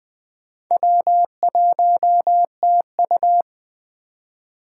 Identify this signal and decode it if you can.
Morse code, heard as a keyed tone, about 20 wpm, 700 Hz. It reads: W1TU
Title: Speed Racing - USA 1x2 Call Signs 17wpm